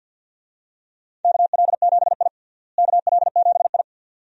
Morse code, heard as a keyed tone, about 51 wpm, 700 Hz. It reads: KL7I KL7I
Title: DXCC-100 Call Signs - 4 Characters - Repeated 2x - 50wpm